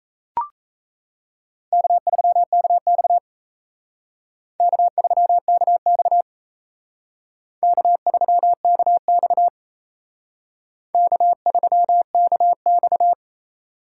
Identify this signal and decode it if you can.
Morse code, heard as a keyed tone, about 35 wpm, 700 Hz. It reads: K3KX K3KX K3KX K3KX
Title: Speed Racing - USA 1x2 Call Signs 28wpm